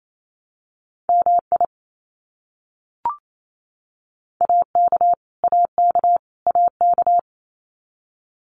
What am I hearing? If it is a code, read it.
Morse code, heard as a keyed tone, about 28 wpm, 700 Hz. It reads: MI AK AK AK
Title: US States Repeated 3x 28wpm